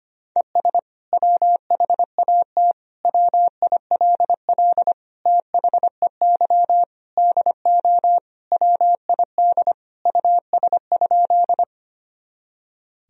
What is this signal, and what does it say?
Morse code, heard as a keyed tone, about 25 wpm, 700 Hz. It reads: ES WHAT WILL THEY DO WID US?
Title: Sentences from Top 200 Words and Abbreviations 25wpm